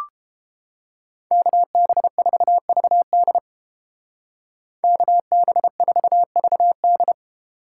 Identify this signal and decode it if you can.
Morse code, heard as a keyed tone, about 29 wpm, 700 Hz. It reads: KB4VD KB4VD